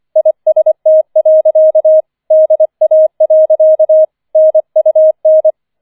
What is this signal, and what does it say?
Morse code, heard as a keyed tone, about 25 wpm, 630 Hz. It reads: IST. DA. NUN